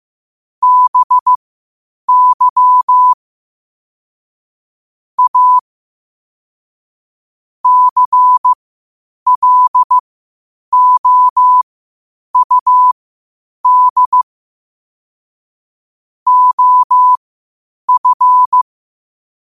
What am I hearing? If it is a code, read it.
Morse code, heard as a keyed tone, about 15 wpm, spaced as for 8 wpm, 1000 Hz. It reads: BY A CLOUD OF
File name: qotdfiles / 8wpm